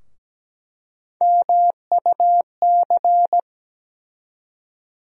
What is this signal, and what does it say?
Morse code, heard as a keyed tone, about 17 wpm, 700 Hz. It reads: MUC
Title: USA Call Sign Postfixes - 3 Letters 17wpm